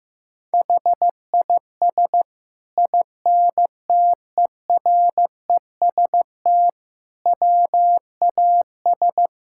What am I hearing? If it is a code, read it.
Morse code, heard as a keyed tone, about 15 wpm, 700 Hz. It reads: HIS INTEREST WAS
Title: Sentences from Top 1000 Words - Review Part 5 - 15wpm